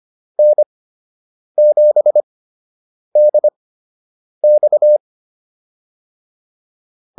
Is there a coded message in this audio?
Yes: Morse code from 0.4 to 5.0 s, heard as a keyed tone, about 25 wpm, spaced as for 8 wpm, 600 Hz. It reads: N7DX